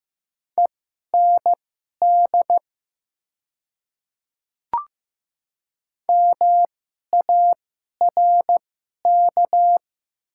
Morse code, heard as a keyed tone, about 15 wpm, spaced as for 11 wpm, 690 Hz. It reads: END MARK
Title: 2x Character Spacing - Top 300 Words - 15wpm